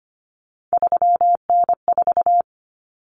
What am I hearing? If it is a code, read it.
Morse code, heard as a keyed tone, about 25 wpm, 700 Hz. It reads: 3N4
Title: Common Cut Numbers - 3-Digit 25wpm